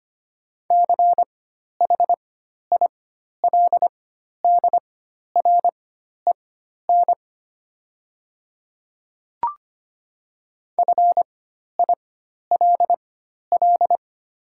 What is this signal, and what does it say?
Morse code, heard as a keyed tone, about 25 wpm, spaced as for 12 wpm, 690 Hz. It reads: CHILDREN FILL
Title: 4x Character Spacing - Top 500 Words - 25wpm